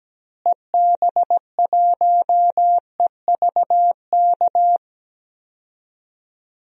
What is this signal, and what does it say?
Morse code, heard as a keyed tone, about 17 wpm, 700 Hz. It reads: EB1EVK
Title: DXCC-100 Call Signs - 6 Characters - Repeated 2x - 17wpm